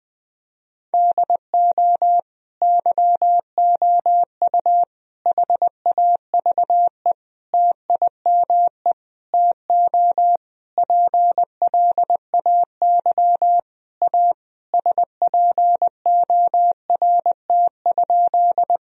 Morse code, heard as a keyed tone, about 20 wpm, 700 Hz. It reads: DO YOU HAVE TIME TO PLAY A SPORT?
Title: Sentences from Top 1000 Words - Review Part 4 - 20wpm